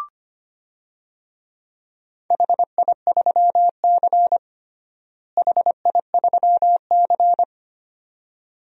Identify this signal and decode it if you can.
Morse code, heard as a keyed tone, about 25 wpm, 700 Hz. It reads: HI3C HI3C
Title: DXCC-100 Call Signs - 4 Characters - Repeated 2x - 25wpm